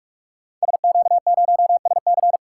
Morse code, heard as a keyed tone, about 46 wpm, 700 Hz. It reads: SQ0SC
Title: Code Groups - 5 Letter:Numbers - 45wpm